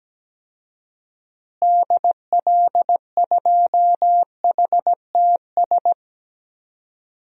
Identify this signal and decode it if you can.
Morse code, heard as a keyed tone, about 17 wpm, 700 Hz. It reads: DL2HTS
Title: DXCC-100 Call Signs - 6 Characters - Repeated 3x - 17wpm